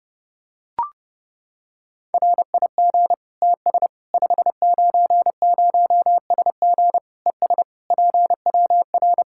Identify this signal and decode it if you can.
Morse code, heard as a keyed tone, about 30 wpm, 700 Hz. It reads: RIG TS 590SG ES PWR
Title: Practice QSO - Exchange 2 - Info - 30wpm